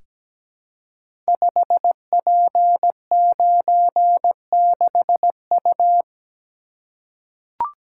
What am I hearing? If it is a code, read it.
Morse code, heard as a keyed tone, about 17 wpm, 700 Hz. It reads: 5P96U
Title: Code Groups - 5 Letter:Numbers - 17wpm